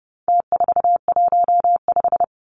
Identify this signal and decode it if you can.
Morse code, heard as a keyed tone, about 29 wpm, 700 Hz. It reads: T415
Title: Numbers - Contest Style - 4-Digit 30wpm